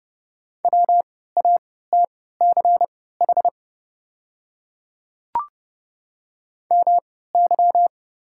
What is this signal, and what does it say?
Morse code, heard as a keyed tone, about 30 wpm, spaced as for 17 wpm, 690 Hz. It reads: WATCH MY
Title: 3x Character Spacing - Top 300 Words - 30wpm